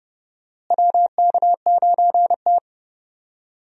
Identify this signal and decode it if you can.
Morse code, heard as a keyed tone, about 30 wpm, 700 Hz. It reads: WK9T
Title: USA Call Signs - 4 Characters - Repeated 2x - 30wpm